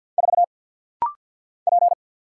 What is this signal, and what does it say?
Morse code, heard as a keyed tone, about 50 wpm, 700 Hz. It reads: 4 P
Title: Single Letter-Number Alt - Rapid-Fire 50wpm